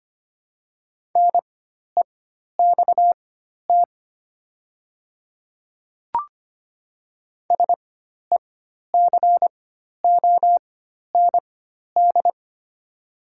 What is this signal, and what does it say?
Morse code, heard as a keyed tone, about 25 wpm, spaced as for 12 wpm, 700 Hz. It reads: NEXT SECOND
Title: 4x Character Spacing - Top 300 Words - 25wpm